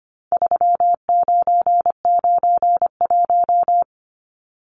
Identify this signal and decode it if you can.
Morse code, heard as a keyed tone, about 25 wpm, 700 Hz. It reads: 3991
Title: Numbers - 4-Digit 25wpm